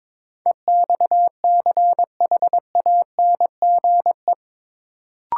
Morse code, heard as a keyed tone, about 22 wpm, 700 Hz. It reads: EXCHANGE
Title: Sentences from Top 1000 Words - Review Part 6 - New Word Drill - 22wpm